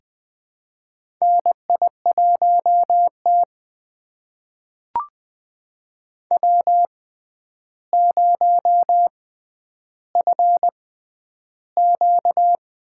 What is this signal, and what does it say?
Morse code, heard as a keyed tone, about 20 wpm, spaced as for 10 wpm, 700 Hz. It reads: NI1T W0FQ
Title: USA Call Signs - 4 Characters - 6x Character Spacing - 20wpm